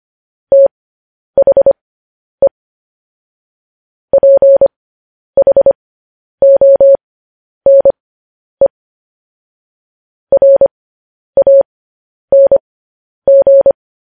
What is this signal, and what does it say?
Morse code, heard as a keyed tone, about 25 wpm, spaced as for 10 wpm, 550 Hz. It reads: THE PHONE RANG
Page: news-headlines-in-morse-code-at-10-wpm